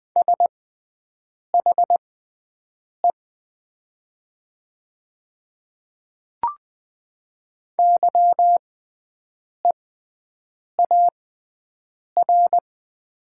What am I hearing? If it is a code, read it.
Morse code, heard as a keyed tone, about 20 wpm, spaced as for 7 wpm, 700 Hz. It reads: SHE YEAR
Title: 6x Character Spacing - Top 200 Words - 20wpm